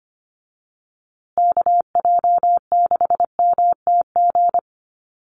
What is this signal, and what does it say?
Morse code, heard as a keyed tone, about 25 wpm, 700 Hz. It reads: KJ6MTG
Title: USA Call Signs Repeated 3x 25wpm